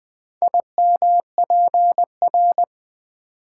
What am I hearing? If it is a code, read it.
Morse code, heard as a keyed tone, about 20 wpm, 700 Hz. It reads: IMPR
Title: Build A Word - 4 Letters At A Time - 10-Letter Words - 20wpm